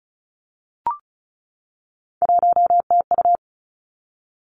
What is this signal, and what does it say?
Morse code, heard as a keyed tone, about 36 wpm, 700 Hz. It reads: 1TU
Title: Three Challenging Letter-Number Combinations 35wpm